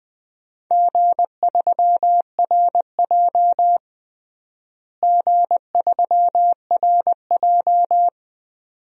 Morse code, heard as a keyed tone, about 20 wpm, 700 Hz. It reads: G3RJ G3RJ
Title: DXCC-100 Call Signs - 4 Characters - Repeated 2x - 20wpm